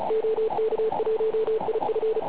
Morse code, heard as a keyed tone, about 36 wpm, 440 Hz, with radio noise. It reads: OK1IF
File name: Pile UP CUC204